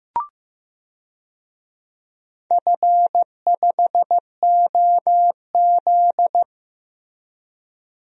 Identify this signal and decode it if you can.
Morse code, heard as a keyed tone, about 15 wpm, 700 Hz. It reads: F5OZ
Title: DXCC-100 Call Signs - 4 Characters - Repeated 2x - 15wpm